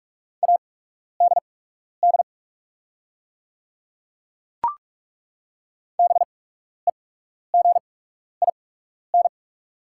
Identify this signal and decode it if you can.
Morse code, heard as a keyed tone, about 41 wpm, spaced as for 12 wpm, 700 Hz. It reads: ADD BEGIN
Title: 8x Character Spacing - Top 300 Words - 45wpm